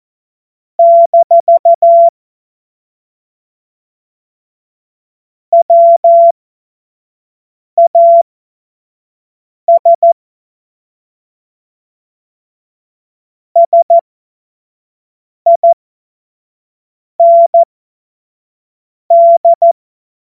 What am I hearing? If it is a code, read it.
Morse code, heard as a keyed tone, about 14 wpm, spaced as for 5 wpm, 680 Hz. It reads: - WAS SIND
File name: Morsekurs / Witze / Witz 26 März 2024 25 F70